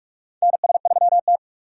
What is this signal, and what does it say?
Morse code, heard as a keyed tone, about 44 wpm, 700 Hz. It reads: NS3T
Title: USA Call Signs - 2x Character Spacing - 45wpm